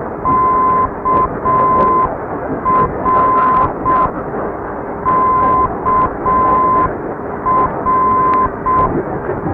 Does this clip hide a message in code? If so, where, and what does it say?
Yes: Morse code from 0.3 to 8.9 s, heard as a keyed tone, about 6 wpm, 1030 Hz. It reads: KR KR